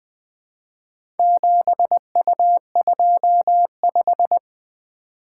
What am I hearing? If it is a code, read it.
Morse code, heard as a keyed tone, about 20 wpm, 700 Hz. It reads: 7U25
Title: Four Letter-Number Combinations - 20wpm